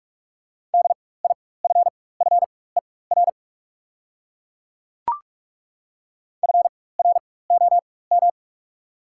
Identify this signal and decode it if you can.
Morse code, heard as a keyed tone, about 42 wpm, spaced as for 20 wpm, 690 Hz. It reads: DIFFER FROM
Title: 4x Character Spacing - Top 200 Words - 45wpm